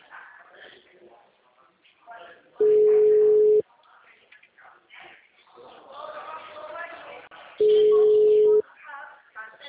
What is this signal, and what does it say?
Ringback tone (Europe, 425 Hz), with speech.